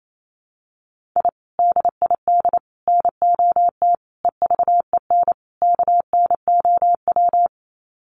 Morse code, heard as a keyed tone, about 28 wpm, 700 Hz. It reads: I DID NOT EVEN KNOW
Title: Sets of 5 Words Encore 28wpm